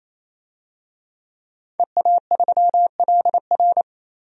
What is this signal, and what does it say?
Morse code, heard as a keyed tone, about 28 wpm, 700 Hz. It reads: EA3LR
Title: DXCC-100 Call Signs - 5 Characters - 28wpm